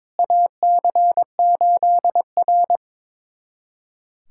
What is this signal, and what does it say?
Morse code, heard as a keyed tone, about 22 wpm, 700 Hz. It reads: AC8R
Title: USA Callsign DE Callsign - 4 Characters - 22wpm